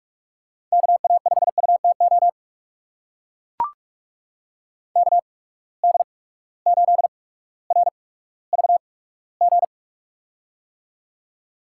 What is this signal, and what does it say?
Morse code, heard as a keyed tone, about 43 wpm, spaced as for 19 wpm, 700 Hz. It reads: KA5UTO KD8RVG
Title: USA Call Signs - 6 Characters - 8x Character Spacing - 45wpm